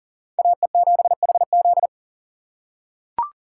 Morse code, heard as a keyed tone, about 38 wpm, 700 Hz. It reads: AE7HZ
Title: USA Call Signs - 6x Character Spacing - 40wpm